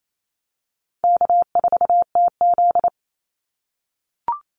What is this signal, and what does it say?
Morse code, heard as a keyed tone, about 28 wpm, 700 Hz. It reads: K4TZ
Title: Call Signs - 28wpm